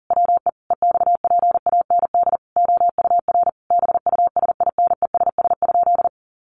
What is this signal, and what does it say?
Morse code, heard as a keyed tone, about 40 wpm, 700 Hz. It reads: WE EXPAND OUR BUSINESS?
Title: Sentences from Top 1400 Words 40wpm